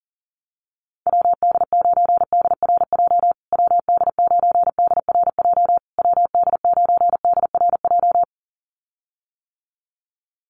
Morse code, heard as a keyed tone, about 40 wpm, 700 Hz. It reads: WD9DRJ WD9DRJ WD9DRJ
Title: USA Call Signs Repeated 3x 40wpm